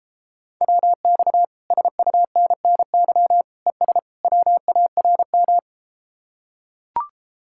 Morse code, heard as a keyed tone, about 33 wpm, 700 Hz. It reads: WX SUNNY ES WARM